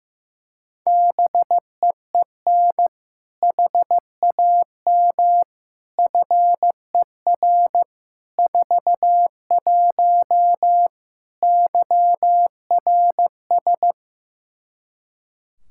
Morse code, heard as a keyed tone, about 15 wpm, 700 Hz. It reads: BEEN HAM FER 41 YRS
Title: Practice QSO - Exchange 3 - More Info - 15wpm